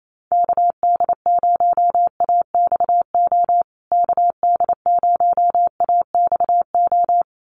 Morse code, heard as a keyed tone, about 28 wpm, 700 Hz. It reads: KD0AXO KD0AXO
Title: USA Call Signs Repeated 2x 28wpm